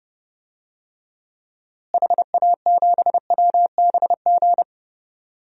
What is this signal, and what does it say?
Morse code, heard as a keyed tone, about 30 wpm, 700 Hz. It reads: HA7WBG
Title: DXCC-100 Call Signs - 6 Characters - Repeated 2x - 30wpm